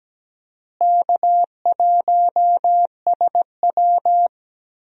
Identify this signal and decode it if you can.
Morse code, heard as a keyed tone, about 17 wpm, 700 Hz. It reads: K1SW
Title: USA Call Signs - 4 Characters - Repeated 3x - 17wpm